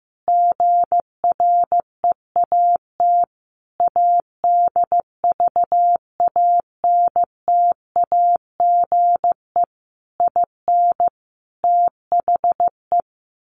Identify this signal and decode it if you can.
Morse code, heard as a keyed tone, about 15 wpm, 700 Hz. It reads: GREAT ADVANTAGE IN THE
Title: Sentences from Top 900 Words 15wpm